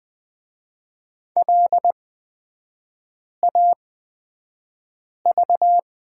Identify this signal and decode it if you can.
Morse code, heard as a keyed tone, about 20 wpm, spaced as for 5 wpm, 700 Hz. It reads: LAV
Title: 236 Lesson 38e - Letter q - Three Characters 20wpm